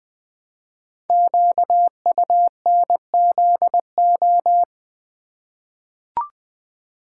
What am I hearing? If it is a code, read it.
Morse code, heard as a keyed tone, about 21 wpm, 700 Hz. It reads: QUNZO
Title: Code Groups - 5 Letters - 20wpm